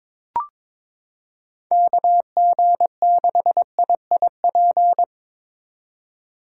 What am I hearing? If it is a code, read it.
Morse code, heard as a keyed tone, about 22 wpm, 700 Hz. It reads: KG6IIP
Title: K1USN SST QSO Practice - 22wpm